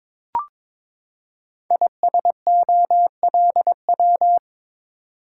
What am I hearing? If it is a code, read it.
Morse code, heard as a keyed tone, about 22 wpm, 700 Hz. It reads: ISOLW
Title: Code Groups - 5 Letter:Numbers - 22wpm